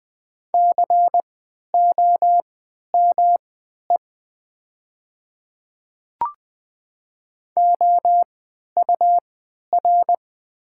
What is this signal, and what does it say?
Morse code, heard as a keyed tone, about 20 wpm, spaced as for 11 wpm, 700 Hz. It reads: COME OUR
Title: 3x Character Spacing - Top 500 Words - 20wpm